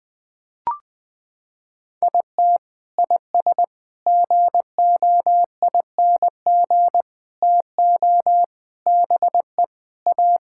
Morse code, heard as a keyed tone, about 20 wpm, 700 Hz. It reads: IT IS GOING TO BE A
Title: Sentences with 4-Word Phrases - 20wpm